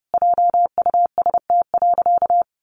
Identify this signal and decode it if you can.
Morse code, heard as a keyed tone, about 30 wpm, 700 Hz. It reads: JUST.
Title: Sentences from Top 800 Words 30wpm